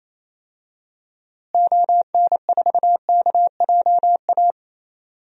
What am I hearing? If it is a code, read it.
Morse code, heard as a keyed tone, about 28 wpm, 700 Hz. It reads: ON4KJA
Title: DXCC-100 Call Signs - 6 Characters - Repeated 2x - 28wpm